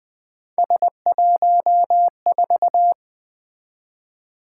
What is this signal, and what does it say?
Morse code, heard as a keyed tone, about 20 wpm, 700 Hz. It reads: S14
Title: Three Letter-Number Combinations - 20wpm